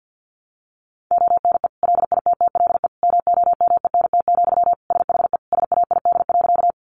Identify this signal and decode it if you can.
Morse code, heard as a keyed tone, about 50 wpm, 700 Hz. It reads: ONE LITTLE MOMENT, SHE SAID.
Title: Sentences from Top 1000 Words 50wpm